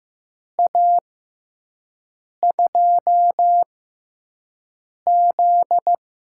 Morse code, heard as a keyed tone, about 15 wpm, spaced as for 5 wpm, 700 Hz. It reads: A2Z
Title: USA Call Signs - 5 Characters - 6x Character Spacing - 15wpm